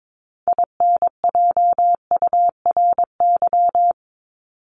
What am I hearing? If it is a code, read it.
Morse code, heard as a keyed tone, about 22 wpm, 700 Hz. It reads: INJURY
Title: Top 1751-2000 Words 22wpm